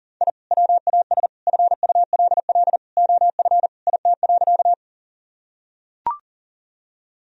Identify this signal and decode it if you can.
Morse code, heard as a keyed tone, about 40 wpm, 700 Hz. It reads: I WAS FULL OF IT.
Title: Sentences from Top 1000 Words - Review Part 3 - 40wpm